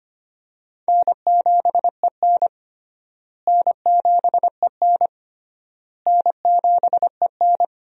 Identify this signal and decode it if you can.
Morse code, heard as a keyed tone, about 25 wpm, 700 Hz. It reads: N7EN N7EN N7EN
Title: USA Call Signs - 4 Characters - Repeated 3x - 25wpm